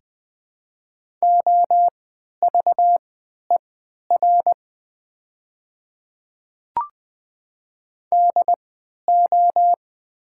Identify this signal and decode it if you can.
Morse code, heard as a keyed tone, about 21 wpm, spaced as for 11 wpm, 700 Hz. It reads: OVER DO
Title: 3x Character Spacing - Top 300 Words - 20wpm